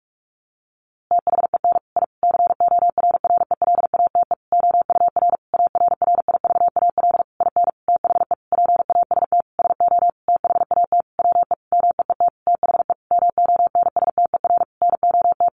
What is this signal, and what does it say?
Morse code, heard as a keyed tone, about 44 wpm, 700 Hz. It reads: THEN I CORRELATE OUR ARRIVAL IN THE PAST SO THAT WE MEET THE MONSTER NOT
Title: A Sound Of Thunder 45wpm